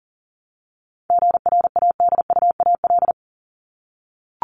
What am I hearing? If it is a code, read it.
Morse code, heard as a keyed tone, about 40 wpm, 700 Hz. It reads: GRADUAL
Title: LY Suffix - Top 100 Words with Base Words 40wpm